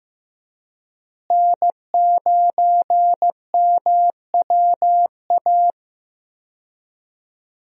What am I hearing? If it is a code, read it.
Morse code, heard as a keyed tone, about 16 wpm, 700 Hz. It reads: N9MWA
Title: USA Call Signs - 5 Characters - Repeated 3x - 15wpm